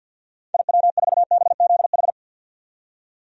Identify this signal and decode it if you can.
Morse code, heard as a keyed tone, about 46 wpm, 700 Hz. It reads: IW4BZH
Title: DXCC-100 Call Signs - 6 Characters - 50wpm